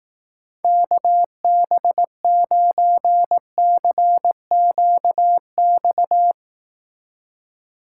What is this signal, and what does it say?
Morse code, heard as a keyed tone, about 19 wpm, 700 Hz. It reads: KB9CQX
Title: Speed Racing - USA 2x3 Call Signs 15wpm